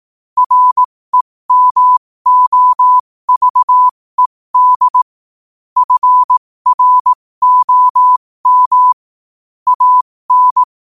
Morse code, heard as a keyed tone, about 18 wpm, spaced as for 15 wpm, 1000 Hz. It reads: REMOVED FROM AN